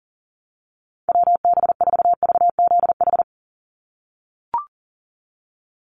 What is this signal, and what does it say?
Morse code, heard as a keyed tone, about 40 wpm, 700 Hz. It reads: WB4VZH
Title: USA Call Signs 40wpm